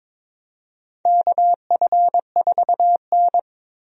Morse code, heard as a keyed tone, about 21 wpm, 700 Hz. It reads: KF4N
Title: USA Call Signs - 4 Characters - Repeated 2x - 22wpm